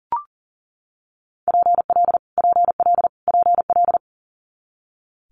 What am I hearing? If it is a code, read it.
Morse code, heard as a keyed tone, about 40 wpm, 700 Hz. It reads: PL PL PL
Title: Multi-letter Phonemes Repeated 3x 40wpm